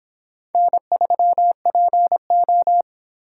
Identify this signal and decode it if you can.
Morse code, heard as a keyed tone, about 29 wpm, 700 Hz. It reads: N3PO